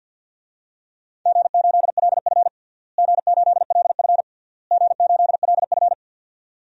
Morse code, heard as a keyed tone, about 48 wpm, 700 Hz. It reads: G8LF G8LF G8LF
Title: DXCC-100 Call Signs - 4 Characters - Repeated 3x - 50wpm